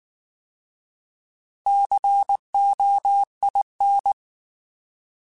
Morse code, heard as a keyed tone, about 19 wpm, 800 Hz. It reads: COIN
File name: Alphabetical / Alphabetical mp3s / Cs @18